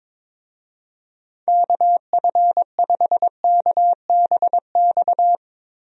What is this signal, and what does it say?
Morse code, heard as a keyed tone, about 22 wpm, 700 Hz. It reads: KF5KBX
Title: USA Call Signs - 6 Characters - Repeated 2x - 22wpm